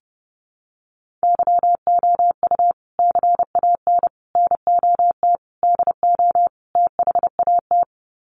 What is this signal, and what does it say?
Morse code, heard as a keyed tone, about 30 wpm, 700 Hz. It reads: YOU CAN NOT DO THAT
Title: Sets of 5 Words 30wpm